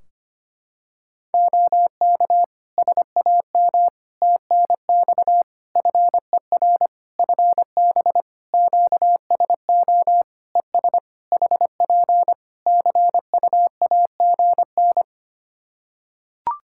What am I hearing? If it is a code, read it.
Morse code, heard as a keyed tone, about 25 wpm, 700 Hz. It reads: OK SAM TNX FER FB QSO ES HP CUAGN
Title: Practice QSO - Exchange 4 - Ending - 25wpm